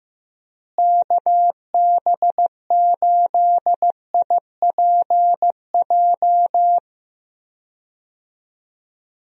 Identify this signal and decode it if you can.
Morse code, heard as a keyed tone, about 15 wpm, 700 Hz. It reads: KB8IPJ
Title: USA Call Signs - 6 Characters - Repeated 3x - 15wpm